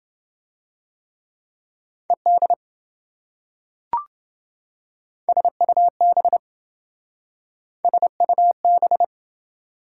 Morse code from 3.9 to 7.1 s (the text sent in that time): SUB